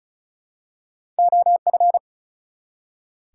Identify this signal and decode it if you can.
Morse code, heard as a keyed tone, about 35 wpm, 700 Hz. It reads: OF
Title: Two Letter Words - 35wpm